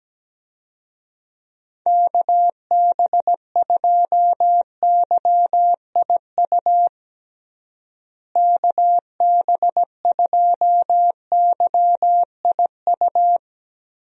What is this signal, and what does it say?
Morse code, heard as a keyed tone, about 17 wpm, 700 Hz. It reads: KB2YIU KB2YIU
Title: USA Call Signs - 6 Characters - Repeated 3x - 17wpm